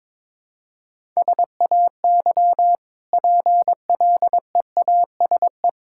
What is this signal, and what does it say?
Morse code, heard as a keyed tone, about 22 wpm, 700 Hz. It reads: SAY PLEASE
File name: practice / 22wpm / Easy Sentences with Hints - 22wpm